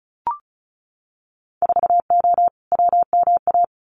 Morse code, heard as a keyed tone, about 36 wpm, 700 Hz. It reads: 4O WMA
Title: Winter Field Day Exchanges Encore 35wpm